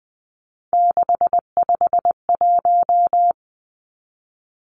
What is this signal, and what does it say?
Morse code, heard as a keyed tone, about 20 wpm, 700 Hz. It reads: 651
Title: Numbers - 3-Digit 20wpm